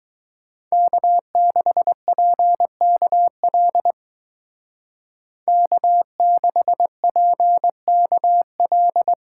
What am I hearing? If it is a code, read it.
Morse code, heard as a keyed tone, about 20 wpm, 700 Hz. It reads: K6PKL K6PKL
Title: Speed Racing - USA 1x3 Call Signs 17wpm